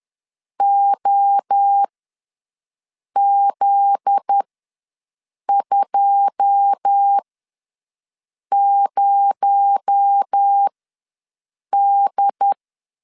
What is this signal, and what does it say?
Morse code, heard as a keyed tone, about 10 wpm, spaced as for 5 wpm, 800 Hz. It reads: OZ20D
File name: LEZIONI / 9 2867 ws 10wpm 4215